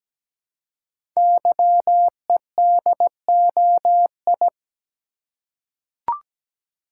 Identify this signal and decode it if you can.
Morse code, heard as a keyed tone, about 17 wpm, 700 Hz. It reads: YEDOI
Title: Code Groups - 5 Letters - 17wpm